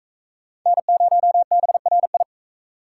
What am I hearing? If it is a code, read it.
Morse code, heard as a keyed tone, about 42 wpm, 700 Hz. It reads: N0BRI